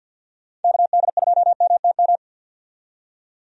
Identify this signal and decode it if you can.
Morse code, heard as a keyed tone, about 52 wpm, 700 Hz. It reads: KD2MTM
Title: USA Call Signs - 6 Characters - 50wpm